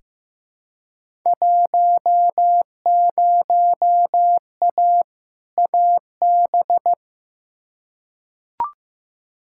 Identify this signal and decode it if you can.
Morse code, heard as a keyed tone, about 15 wpm, 700 Hz. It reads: 10A AB
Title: Field Day Exchanges Encore - 15wpm